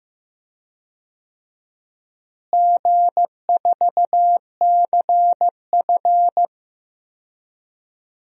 Morse code, heard as a keyed tone, about 15 wpm, 700 Hz. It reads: G4CF
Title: DXCC-100 Call Signs - 4 Characters - Repeated 3x - 15wpm